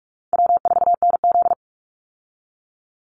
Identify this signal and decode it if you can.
Morse code, heard as a keyed tone, about 44 wpm, 700 Hz. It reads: W4NZ
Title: Call Signs Repeated 3x - 45wpm